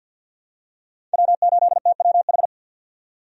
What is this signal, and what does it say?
Morse code, heard as a keyed tone, about 52 wpm, 700 Hz. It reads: W8TWH
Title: USA Call Signs - 5 Characters - 50wpm